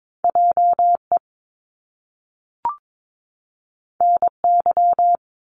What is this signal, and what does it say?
Morse code, heard as a keyed tone, about 22 wpm, 700 Hz. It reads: JE NY
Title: Top Two Letter Combinations in Words 22wpm